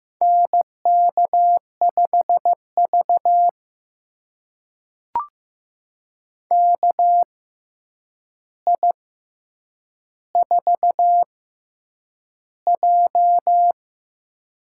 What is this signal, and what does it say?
Morse code, heard as a keyed tone, about 15 wpm, spaced as for 8 wpm, 700 Hz. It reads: NK5V KI4J
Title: USA Call Signs - 4 Characters - 6x Character Spacing - 15wpm